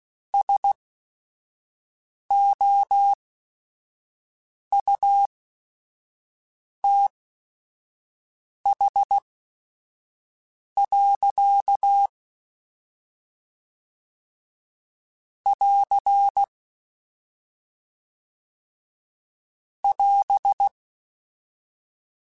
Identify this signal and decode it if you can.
Morse code, heard as a keyed tone, about 16 wpm, spaced as for 5 wpm, 800 Hz. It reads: SOUTH. <AR> <AS>
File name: Weather Sets / Weather Sets-mp3s / Weather 4 10 @5